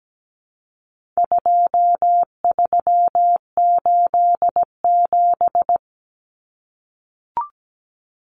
Morse code, heard as a keyed tone, about 17 wpm, 700 Hz. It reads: 2387
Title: Common Cut Numbers - 4-Digit 17wpm